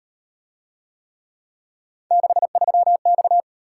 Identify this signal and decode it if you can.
Morse code, heard as a keyed tone, about 35 wpm, 700 Hz. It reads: 63X